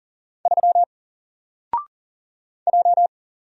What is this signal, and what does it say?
Morse code, heard as a keyed tone, about 40 wpm, 700 Hz. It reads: 3 J
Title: Single Character (Letter:Number:Punctuation:Prosign) Alt 40wpm